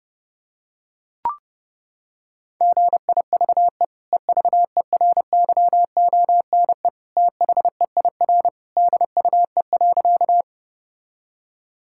Morse code, heard as a keyed tone, about 32 wpm, 700 Hz. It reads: GIVE EVERYONE THEIR DUE.
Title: Sentences from Top 1000 Words - Review Part 6 - 30wpm